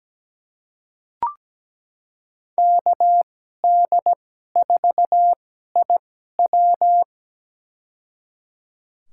Morse code, heard as a keyed tone, about 17 wpm, spaced as for 12 wpm, 700 Hz. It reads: KD4IW
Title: USA Call Signs - 5 Characters - 2x Character Spacing - 17wpm